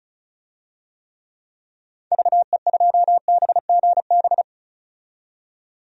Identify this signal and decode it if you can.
Morse code, heard as a keyed tone, about 34 wpm, 700 Hz. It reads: VE2BGB
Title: DXCC-100 Call Signs - 6 Characters - 35wpm